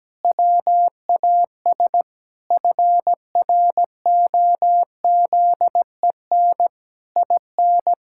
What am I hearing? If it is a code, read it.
Morse code, heard as a keyed tone, about 17 wpm, 700 Hz. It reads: WAS FROZEN IN
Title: Pangram Challenge v2 - 17wpm